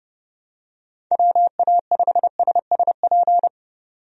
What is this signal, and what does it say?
Morse code, heard as a keyed tone, about 31 wpm, 700 Hz. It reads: WA5SSP
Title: USA Call Signs - 6 Characters - 30wpm